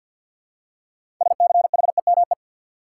Morse code, heard as a keyed tone, about 47 wpm, 700 Hz. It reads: SCHEME
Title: Six Letter Words - 50wpm